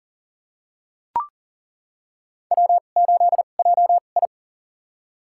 Morse code, heard as a keyed tone, about 40 wpm, spaced as for 29 wpm, 700 Hz. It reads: W8JI
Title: USA Call Signs - 4 Characters - 2x Character Spacing - 40wpm